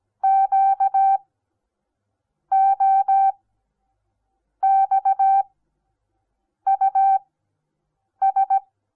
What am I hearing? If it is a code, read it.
Morse code, heard as a keyed tone, about 17 wpm, spaced as for 6 wpm, 780 Hz. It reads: QOXUS
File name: Onlinelehrgang / morsen / DJ4UF Morsen08